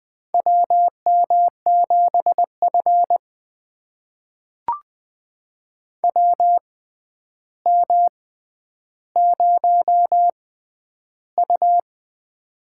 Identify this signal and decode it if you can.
Morse code, heard as a keyed tone, about 20 wpm, spaced as for 10 wpm, 700 Hz. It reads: WM7F WM0U
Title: USA Call Signs - 4 Characters - 6x Character Spacing - 20wpm